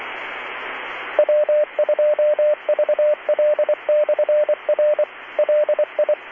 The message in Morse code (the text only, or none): W2VL/R LI